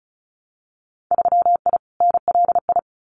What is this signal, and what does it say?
Morse code, heard as a keyed tone, about 36 wpm, 700 Hz. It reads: 3I NLI
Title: Winter Field Day Exchanges Encore 35wpm